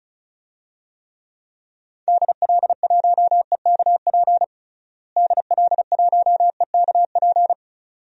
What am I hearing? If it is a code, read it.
Morse code, heard as a keyed tone, about 36 wpm, 700 Hz. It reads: DL1EKP DL1EKP
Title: DXCC-100 Call Signs - 6 Characters - Repeated 2x - 35wpm